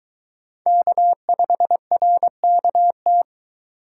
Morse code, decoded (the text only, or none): K5RKT